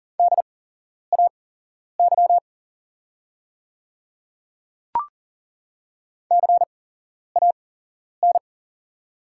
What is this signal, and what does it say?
Morse code, heard as a keyed tone, about 40 wpm, spaced as for 11 wpm, 700 Hz. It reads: DAY CAN
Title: 8x Character Spacing - Top 100 Words - 40wpm